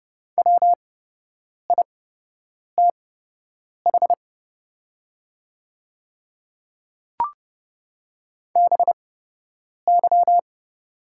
Morse code, heard as a keyed tone, about 29 wpm, spaced as for 8 wpm, 700 Hz. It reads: WITH BY
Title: 8x Character Spacing - Top 100 Words - 30wpm